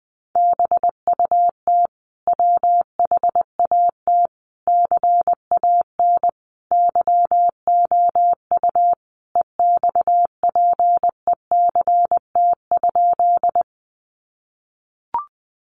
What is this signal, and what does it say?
Morse code, heard as a keyed tone, about 20 wpm, 700 Hz. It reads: BUT WHAT CAN YOU EXPECT?
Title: Sentences from Top 1100 Words 20wpm